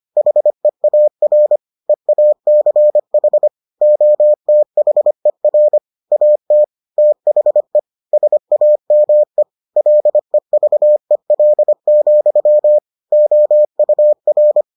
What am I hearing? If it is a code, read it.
Morse code, heard as a keyed tone, about 25 wpm, 600 Hz. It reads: HEAR EACH OTHER AT THE SAME LEVEL, OUR